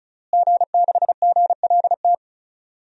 Morse code, heard as a keyed tone, about 36 wpm, 700 Hz. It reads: G6GLT
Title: DXCC-100 Call Signs - 5 Characters - Repeated 2x - 35wpm